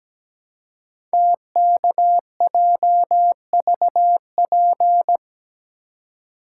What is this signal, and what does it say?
Morse code, heard as a keyed tone, about 17 wpm, 700 Hz. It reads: TKJVP
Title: Code Groups - 5 Letters - 17wpm